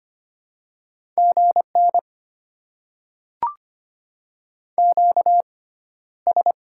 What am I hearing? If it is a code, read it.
Morse code, heard as a keyed tone, about 25 wpm, 700 Hz. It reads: G N Q S
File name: Top 90 QSO Elements - 6x Character Spacing - 25wpm